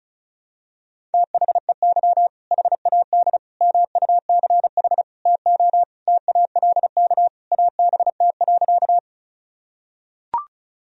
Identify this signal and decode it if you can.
Morse code, heard as a keyed tone, about 36 wpm, 700 Hz. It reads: THEY HAD MUCH TO TALK ABT.
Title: Sentences from Top 400 Words and Abbreviations 35wpm